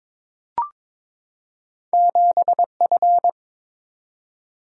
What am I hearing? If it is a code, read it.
Morse code, heard as a keyed tone, about 23 wpm, 700 Hz. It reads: 7F
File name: Two Letter-Number Combinations - 22wpm